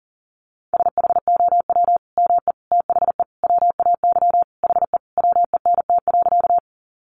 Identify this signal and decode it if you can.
Morse code, heard as a keyed tone, about 40 wpm, 700 Hz. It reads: SHOW ME THE WAY HE WENT.
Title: Sentences from Top 200 Words 40wpm